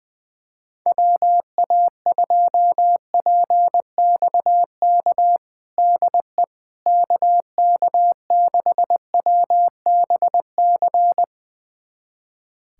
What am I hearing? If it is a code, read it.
Morse code, heard as a keyed tone, about 21 wpm, 700 Hz. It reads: WA2PXK DE KK6WBC
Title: USA Callsign DE Callsign - 6 Characters - 20wpm